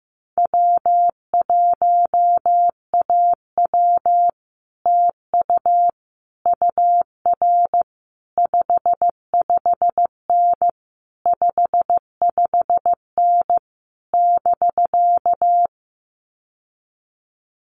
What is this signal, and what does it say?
Morse code, heard as a keyed tone, about 15 wpm, 700 Hz. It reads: W1AW TU UR 55N 55N <BK>
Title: Common QSO Phrases Encore 15wpm